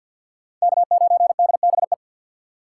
Morse code, heard as a keyed tone, about 62 wpm, 700 Hz. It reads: K9DBE
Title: Speed Racing - USA 1x3 Call Signs 50wpm